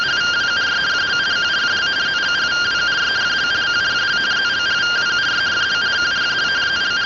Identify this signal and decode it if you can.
Two-tone teleprinter signal, about 170 Hz shift.